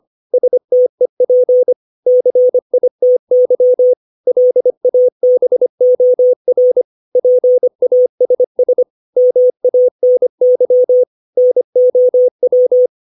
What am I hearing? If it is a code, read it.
Morse code, heard as a keyed tone, about 25 wpm, 500 Hz. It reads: STEP CITY LABOR PASS MANY NOW
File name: 632 words / 632 Words 25WPM